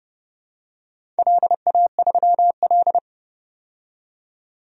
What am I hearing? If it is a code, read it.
Morse code, heard as a keyed tone, about 30 wpm, 700 Hz. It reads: LA3L
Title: DXCC-100 Call Signs - 4 Characters - 30wpm